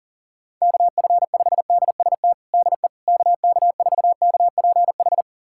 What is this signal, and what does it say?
Morse code, heard as a keyed tone, about 40 wpm, 700 Hz. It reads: KF5DST DE KK4KPH
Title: USA Callsign DE Callsign 40wpm